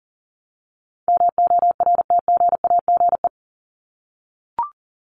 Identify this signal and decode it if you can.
Morse code, heard as a keyed tone, about 40 wpm, 700 Hz. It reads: MORTGAGE
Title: Top 1751-2000 Words 40wpm